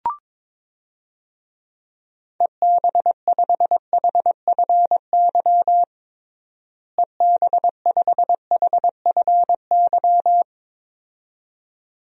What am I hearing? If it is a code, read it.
Morse code, heard as a keyed tone, about 22 wpm, 700 Hz. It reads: EB5HFY EB5HFY
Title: DXCC-100 Call Signs - 6 Characters - Repeated 2x - 22wpm